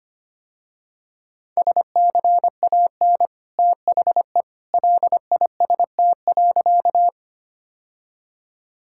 Morse code, heard as a keyed tone, about 25 wpm, 700 Hz. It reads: SCAN THE LIST.
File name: Easy Sentences with Hints - 25wpm